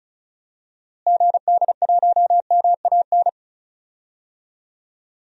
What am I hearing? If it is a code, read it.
Morse code, heard as a keyed tone, about 34 wpm, 700 Hz. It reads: GD1MAN
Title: DXCC-100 Call Signs - 6 Characters - Repeated 3x - 35wpm